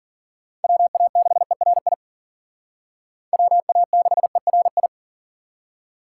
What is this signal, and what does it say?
Morse code, heard as a keyed tone, about 52 wpm, 700 Hz. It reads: WA6ERI WA6ERI